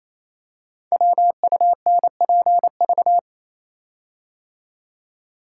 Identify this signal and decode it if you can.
Morse code, heard as a keyed tone, about 27 wpm, 700 Hz. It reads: WUNPV
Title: Code Groups - 5 Letters - 28wpm